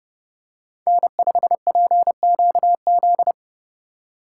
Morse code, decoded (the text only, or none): N5PQZ